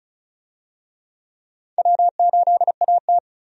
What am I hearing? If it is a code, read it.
Morse code, heard as a keyed tone, about 36 wpm, 700 Hz. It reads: W8AT
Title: USA Call Signs - 4 Characters - Repeated 2x - 35wpm